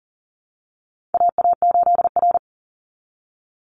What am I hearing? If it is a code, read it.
Morse code, heard as a keyed tone, about 40 wpm, 700 Hz. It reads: AA8R
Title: Call Signs Repeated 3x - 40wpm